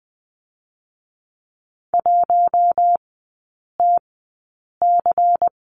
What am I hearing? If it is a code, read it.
Morse code, heard as a keyed tone, about 20 wpm, spaced as for 8 wpm, 690 Hz. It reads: 1TC
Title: 212 Lesson 34e - Number 8 - Three Characters 20wpm